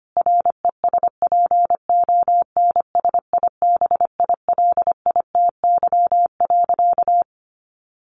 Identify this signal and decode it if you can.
Morse code, heard as a keyed tone, about 25 wpm, 700 Hz. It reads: RESPONSIBILITY.
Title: Sentences from Top 1200 Words 25wpm